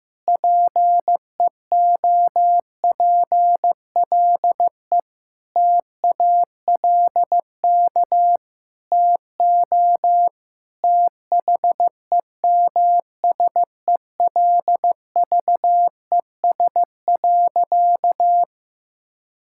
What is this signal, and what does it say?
Morse code, heard as a keyed tone, about 15 wpm, 700 Hz. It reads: PEOPLE TALK TO THEMSELVES.
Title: Sentences from Top 1000 Words - Review Part 5 - 15wpm